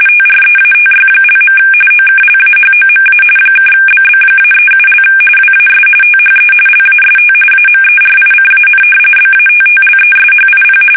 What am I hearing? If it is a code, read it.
RTTY, about 45.45 baud. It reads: TREATY). OKLAHOMA (BB-37) WAS THE LAST AMERICAN BATTLESHIP
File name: fsk850 / battleships